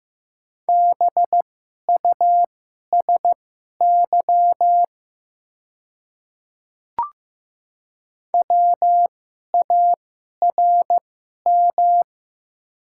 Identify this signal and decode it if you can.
Morse code, heard as a keyed tone, about 15 wpm, spaced as for 11 wpm, 700 Hz. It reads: BUSY WARM
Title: 2x Character Spacing - Top 500 Words - 15wpm